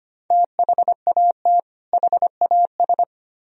Morse code, heard as a keyed tone, about 33 wpm, 700 Hz. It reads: THAT HAS